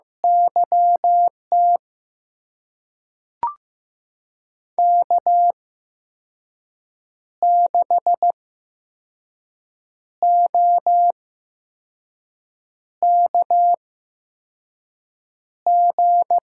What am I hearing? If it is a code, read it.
Morse code, heard as a keyed tone, about 15 wpm, spaced as for 5 wpm, 700 Hz. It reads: YT K6OKG
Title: USA Call Signs - 8x Character Spacing - 15wpm